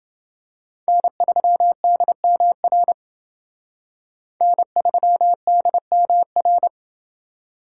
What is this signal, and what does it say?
Morse code, heard as a keyed tone, about 28 wpm, 700 Hz. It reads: N3DMR N3DMR